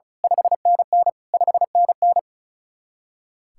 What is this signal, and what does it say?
Morse code, heard as a keyed tone, about 33 wpm, 700 Hz. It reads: 5NN 5NN
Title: Speed-Racing Common QSO Phrases - 35wpm